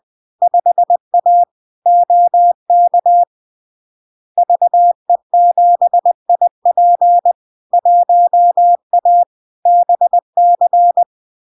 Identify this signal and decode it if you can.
Morse code, heard as a keyed tone, about 20 wpm, 700 Hz. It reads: 5A OK VE7IP 1A BC
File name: LARK-FD-2022-Log-in-CW-20wpm-700-Hz